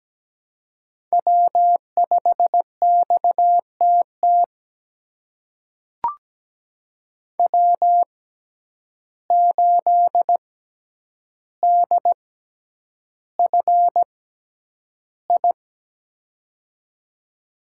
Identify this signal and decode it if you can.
Morse code, heard as a keyed tone, about 17 wpm, spaced as for 9 wpm, 700 Hz. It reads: W5XTT W8DFI
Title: USA Call Signs - 5 Characters - 6x Character Spacing - 17wpm